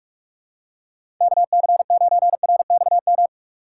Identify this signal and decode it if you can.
Morse code, heard as a keyed tone, about 45 wpm, 700 Hz. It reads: KC9RXM
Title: USA Call Signs - 6 Characters - 6x Character Spacing - 45wpm